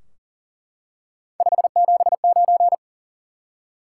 Morse code, heard as a keyed tone, about 41 wpm, 700 Hz. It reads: 579
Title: Top 180 QSO Elements 40wpm